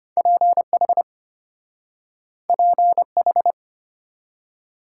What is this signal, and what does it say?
Morse code, heard as a keyed tone, about 34 wpm, 700 Hz. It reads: PH PH